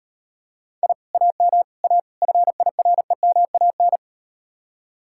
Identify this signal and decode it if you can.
Morse code, heard as a keyed tone, about 39 wpm, 700 Hz. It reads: I AM A FIREMAN